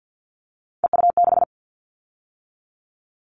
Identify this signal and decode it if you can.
Morse code, heard as a keyed tone, about 50 wpm, 690 Hz. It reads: EU6
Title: Callsign Prefixes for Top 100 DXCC - 50wpm